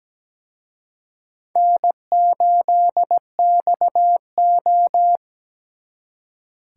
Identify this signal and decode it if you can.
Morse code, heard as a keyed tone, about 20 wpm, 700 Hz. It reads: N8XO
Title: Speed Racing - USA 1x2 Call Signs 17wpm